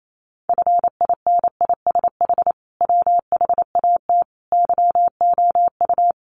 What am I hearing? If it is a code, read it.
Morse code, heard as a keyed tone, about 28 wpm, 700 Hz. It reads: FINISH WHAT YOU
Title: Sentences from Top 1200 Words 28wpm